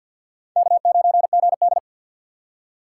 Morse code, heard as a keyed tone, about 64 wpm, 700 Hz. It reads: K9GD